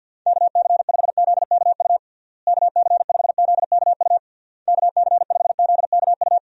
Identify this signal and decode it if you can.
Morse code, heard as a keyed tone, about 53 wpm, 700 Hz. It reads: KC5ZKU KC5ZKU KC5ZKU
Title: USA Call Signs - 6 Characters - Repeated 3x - 50wpm